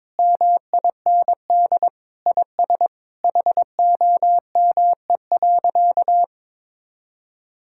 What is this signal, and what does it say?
Morse code, heard as a keyed tone, about 21 wpm, 700 Hz. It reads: MIND IS HOME.
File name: Sentences with 7-Word Phrases - 22wpm